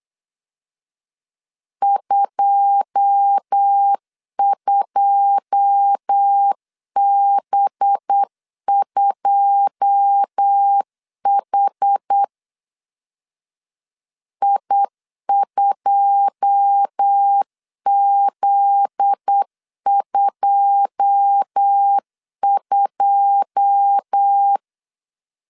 Morse code, heard as a keyed tone, about 8 wpm, 800 Hz. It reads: 22B2H I2Z22
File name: LEZIONI / 9 2 8wpm 3437